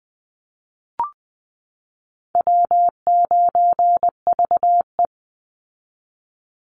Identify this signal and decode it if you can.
Morse code, heard as a keyed tone, about 20 wpm, 700 Hz. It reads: W9VE
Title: Easy Call Signs Repeated 3x - 20wpm